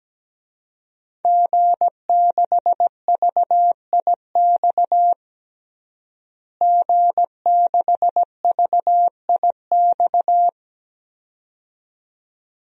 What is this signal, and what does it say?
Morse code, heard as a keyed tone, about 17 wpm, 700 Hz. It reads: G6VIX G6VIX
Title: DXCC-100 Call Signs - 5 Characters - Repeated 2x - 17wpm